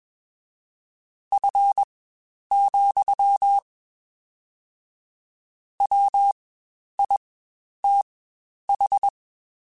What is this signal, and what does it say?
Morse code, heard as a keyed tone, about 21 wpm, spaced as for 10 wpm, 800 Hz. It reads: F, WITH
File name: Spaced Characters / Spaced Characters-mp3s / Weather 4 10 sp @20